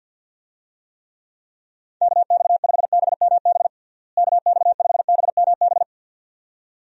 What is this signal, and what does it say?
Morse code, heard as a keyed tone, about 52 wpm, 700 Hz. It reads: KX5BMB KX5BMB
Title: USA Call Signs - 6 Characters - Repeated 2x - 50wpm